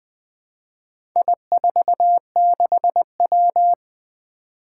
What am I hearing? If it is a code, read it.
Morse code, heard as a keyed tone, about 20 wpm, 700 Hz. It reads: I46W
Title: Four Challenging Letter-Number Combinations 20wpm